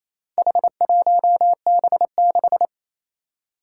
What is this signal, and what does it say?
Morse code, heard as a keyed tone, about 28 wpm, 700 Hz. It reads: H1B6
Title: Four Challenging Letter-Number Combinations 28wpm